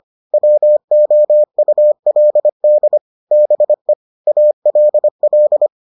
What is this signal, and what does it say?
Morse code, heard as a keyed tone, about 25 wpm, 600 Hz. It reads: WOULD BE ALL